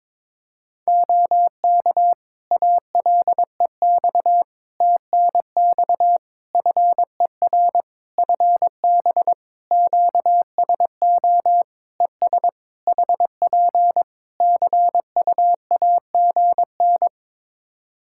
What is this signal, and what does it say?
Morse code, heard as a keyed tone, about 21 wpm, 700 Hz. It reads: OK ALEX TNX FER FB QSO ES HP CUAGN
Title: Practice QSO - Exchange 4 - Ending - 22wpm